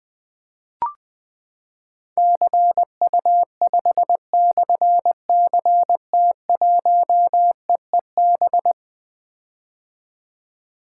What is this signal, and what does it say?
Morse code, heard as a keyed tone, about 20 wpm, 700 Hz. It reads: CU5/CT1EEB
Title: Difficult Call Signs 20wpm